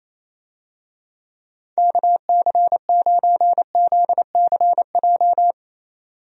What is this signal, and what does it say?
Morse code, heard as a keyed tone, about 28 wpm, 700 Hz. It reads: KC9ZCJ
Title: USA Call Signs - 6 Characters - 28wpm